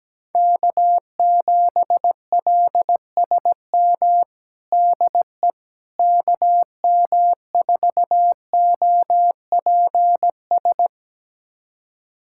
Morse code, heard as a keyed tone, about 17 wpm, 700 Hz. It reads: K7LSM DE KM4OPS
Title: USA Callsign DE Callsign 17wpm